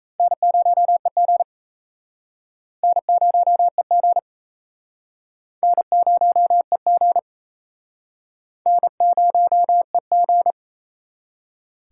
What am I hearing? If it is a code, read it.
Morse code, heard as a keyed tone, about 35 wpm, 700 Hz. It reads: N0EG N0EG N0EG N0EG